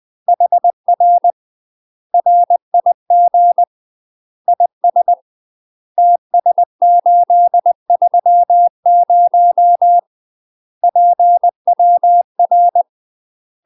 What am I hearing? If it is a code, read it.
Morse code, heard as a keyed tone, about 20 wpm, 700 Hz. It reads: HR RIG IS TS830 PWR
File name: UK MASTER QSOs TRAINING / 20WPM / UK MASTER QSOs 0058